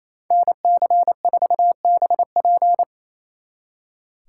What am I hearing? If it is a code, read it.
Morse code, heard as a keyed tone, about 34 wpm, 700 Hz. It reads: NC4BP
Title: Speed Racing - USA 2x2 Call Signs 28wpm